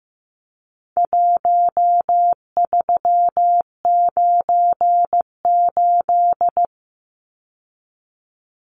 Morse code, heard as a keyed tone, about 15 wpm, 700 Hz. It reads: 1398
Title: Numbers - Contest Style - 4-Digit 15wpm